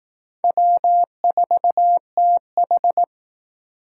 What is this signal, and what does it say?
Morse code, heard as a keyed tone, about 19 wpm, 700 Hz. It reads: W4TH